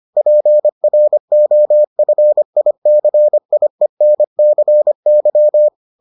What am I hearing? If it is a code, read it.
Morse code, heard as a keyed tone, about 25 wpm, 600 Hz. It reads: PROFICIENCY